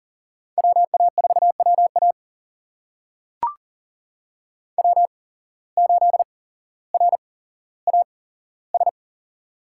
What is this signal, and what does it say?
Morse code, heard as a keyed tone, about 40 wpm, spaced as for 17 wpm, 700 Hz. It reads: WA4WA W8RAS
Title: USA Call Signs - 5 Characters - 8x Character Spacing - 40wpm